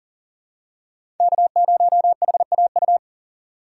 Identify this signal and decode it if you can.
Morse code, heard as a keyed tone, about 39 wpm, 700 Hz. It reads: K0HAU
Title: USA Call Signs - 5 Characters - 6x Character Spacing - 40wpm